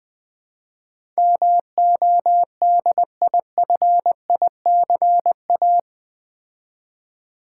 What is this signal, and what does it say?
Morse code, heard as a keyed tone, about 20 wpm, 700 Hz. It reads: MODIFICA
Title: Build A Word - 4 Letters At A Time - 12-Letter Words - 20wpm